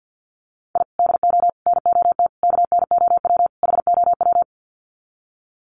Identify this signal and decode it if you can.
Morse code, heard as a keyed tone, about 50 wpm, 700 Hz. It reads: I DO NOT KNOW HOW
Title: Sets of 5 Words 50wpm